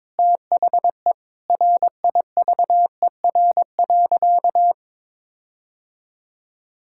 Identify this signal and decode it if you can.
Morse code, heard as a keyed tone, about 22 wpm, 700 Hz. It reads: THE RIVER.
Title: Sentences from Top 1600 Words 22wpm